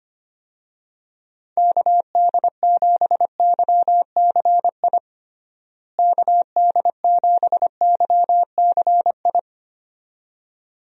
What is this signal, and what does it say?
Morse code, heard as a keyed tone, about 25 wpm, 700 Hz. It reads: KD7YCI KD7YCI
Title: USA Call Signs - 6 Characters - Repeated 2x - 25wpm